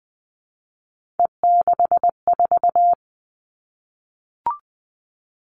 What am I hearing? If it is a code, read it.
Morse code, heard as a keyed tone, about 19 wpm, 700 Hz. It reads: E64
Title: Three Challenging Letter-Number Combinations 20wpm